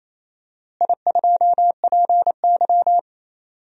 Morse code, heard as a keyed tone, about 28 wpm, 700 Hz. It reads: I2PY
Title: DXCC-100 Call Signs - 4 Characters - Repeated 3x - 28wpm